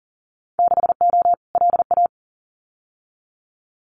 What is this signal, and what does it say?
Morse code, heard as a keyed tone, about 40 wpm, 700 Hz. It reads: 6O LA
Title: Winter Field Day Exchanges Encore 40wpm